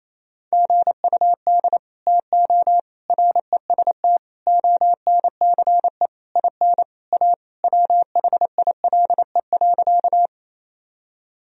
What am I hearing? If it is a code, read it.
Morse code, heard as a keyed tone, about 28 wpm, 700 Hz. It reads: GUD TO REST ONCE IN A WHILE.
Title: Sentences from Top 500 Words and Abbreviations 28wpm